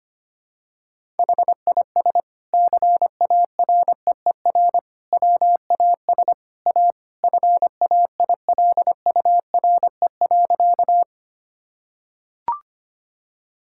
Morse code, heard as a keyed tone, about 25 wpm, 700 Hz. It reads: HIS CAREER WAS A FAILURE.
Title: Sentences from Top 1000 Words - Review Part 6 - 25wpm